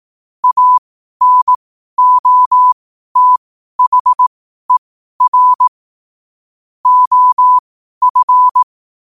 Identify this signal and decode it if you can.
Morse code, heard as a keyed tone, about 18 wpm, spaced as for 12 wpm, 1000 Hz. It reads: ANOTHER OF
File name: qotdfiles / 12wpm